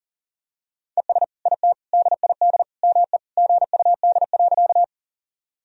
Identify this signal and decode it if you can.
Morse code, heard as a keyed tone, about 40 wpm, 700 Hz. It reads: ES IT DID ME GUD.
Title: Sentences from Top 200 Words and Abbreviations 40wpm